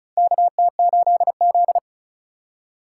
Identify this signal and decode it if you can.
Morse code, heard as a keyed tone, about 36 wpm, 700 Hz. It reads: KT8Z
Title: USA Call Signs - 3x Character Spacing - 35wpm